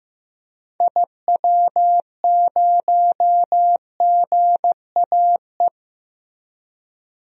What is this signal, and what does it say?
Morse code, heard as a keyed tone, about 15 wpm, 700 Hz. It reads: IW0GAE
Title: DXCC-100 Call Signs - 6 Characters - 15wpm